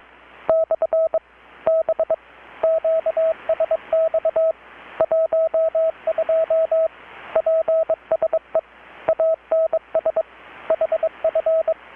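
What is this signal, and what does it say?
Morse code, heard as a keyed tone, about 21 wpm, 650 Hz, with radio noise. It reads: / B QSX 12 PSE ANS HF
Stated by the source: Morse from KPH's Night of Nights XX maritime radiotelegraphy event, received in Chiapas, Mexico about 3,800 km away on a Kenwood R-600 with a grounded 10 m long-wire antenna.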